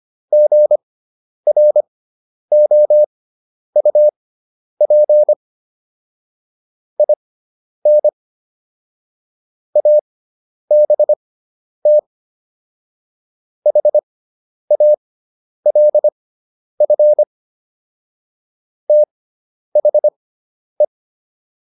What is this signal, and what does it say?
Morse code, heard as a keyed tone, about 25 wpm, spaced as for 10 wpm, 600 Hz. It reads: GROUP IN ABT HALF THE